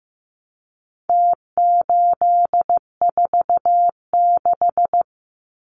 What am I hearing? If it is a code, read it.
Morse code, heard as a keyed tone, about 15 wpm, 700 Hz. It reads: T846
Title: Numbers - Contest Style - 4-Digit 15wpm